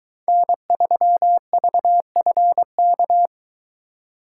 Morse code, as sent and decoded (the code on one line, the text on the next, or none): -. ...-- ...- ..-. -.-
N3VFK